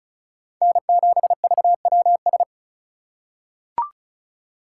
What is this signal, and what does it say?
Morse code, heard as a keyed tone, about 36 wpm, 700 Hz. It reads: N7VWS
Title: USA Call Signs - 5 Characters - 2x Character Spacing - 35wpm